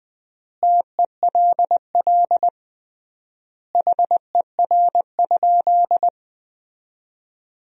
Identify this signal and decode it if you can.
Morse code, heard as a keyed tone, about 20 wpm, 700 Hz. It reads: TELL HER?
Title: 3x Word-Spacing - Sentences from Top 200 Words - 20wpm